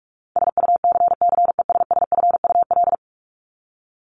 Morse code, heard as a keyed tone, about 44 wpm, 700 Hz. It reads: SUCCESSFUL
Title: Top 751-1000 Words 45wpm